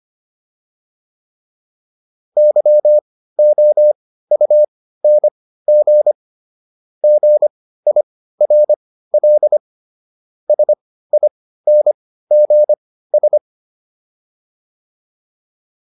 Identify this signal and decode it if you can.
Morse code, heard as a keyed tone, about 25 wpm, spaced as for 15 wpm, 600 Hz. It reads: YOUNG GIRL SINGS